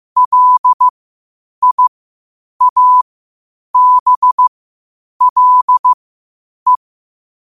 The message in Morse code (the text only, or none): LIABLE